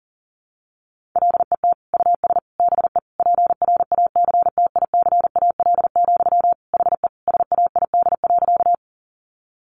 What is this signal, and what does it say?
Morse code, heard as a keyed tone, about 40 wpm, 700 Hz. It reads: LET US BE PRACTICAL, HE SAID.
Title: Sentences from Top 1400 Words 40wpm